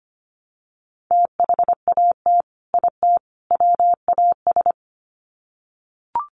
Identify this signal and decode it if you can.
Morse code, heard as a keyed tone, about 25 wpm, 700 Hz. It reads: THAT IT WAS
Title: Sets of 3 Words 25wpm